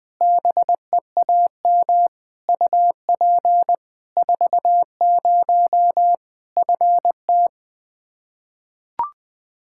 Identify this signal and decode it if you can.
Morse code, heard as a keyed tone, about 20 wpm, 700 Hz. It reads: BEAM UP 40 FT
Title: K1USN SST QSO Practice 20wpm